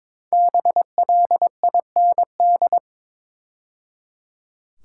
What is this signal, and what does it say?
Morse code, heard as a keyed tone, about 22 wpm, 700 Hz. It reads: BLIND
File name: 5 Letter Words - First 3 Letters Given - 22wpm